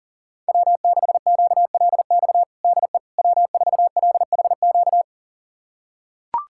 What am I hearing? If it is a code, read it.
Morse code, heard as a keyed tone, about 40 wpm, 700 Hz. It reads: W6QLX DE W4LHQ
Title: USA Callsign DE Callsign - 5 Characters - 40wpm